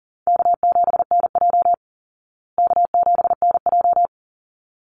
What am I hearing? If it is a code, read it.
Morse code, heard as a keyed tone, about 40 wpm, 700 Hz. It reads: K7NJ K7NJ
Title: Call Signs Repeated 3x - 40wpm